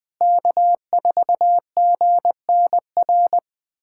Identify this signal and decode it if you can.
Morse code, heard as a keyed tone, about 21 wpm, 700 Hz. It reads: K4GNR
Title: USA Callsign DE Callsign - 5 Characters - 20wpm